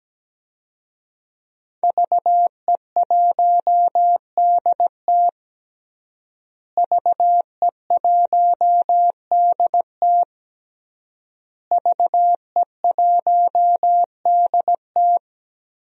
Morse code, heard as a keyed tone, about 17 wpm, 700 Hz. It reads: VE1DT VE1DT VE1DT
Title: DXCC-100 Call Signs - 5 Characters - Repeated 3x - 17wpm